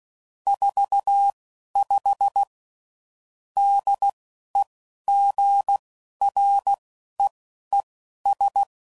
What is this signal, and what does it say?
Morse code, heard as a keyed tone, about 16 wpm, spaced as for 11 wpm, 800 Hz. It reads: 45 DEGREES
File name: CW Code Groups @10WPM / Weather Sets / Weather 3 17